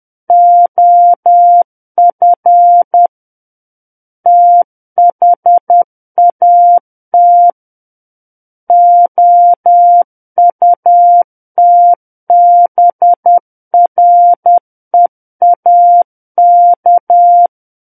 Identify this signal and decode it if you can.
Morse code, heard as a keyed tone, about 10 wpm, 700 Hz. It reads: OF THAT OUTBREAK
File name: Book One / 01-Chapter 1 - 10 WPM